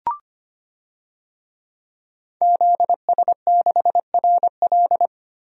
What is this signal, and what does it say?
Morse code, heard as a keyed tone, about 25 wpm, 700 Hz. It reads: ZS6RL
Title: DXCC-100 Call Signs - 5 Characters - Repeated 3x - 25wpm